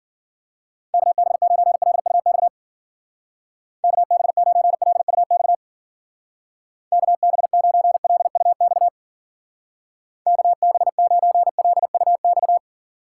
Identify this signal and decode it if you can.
Morse code, heard as a keyed tone, about 50 wpm, 700 Hz. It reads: KB9LUX KB9LUX KB9LUX KB9LUX
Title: Speed Racing - USA 2x3 Call Signs 40wpm